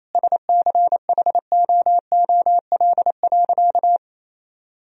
Morse code, heard as a keyed tone, about 28 wpm, 700 Hz. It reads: SCHOOL.
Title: Sentences from Top 1000 Words - Review Part 1 - 28wpm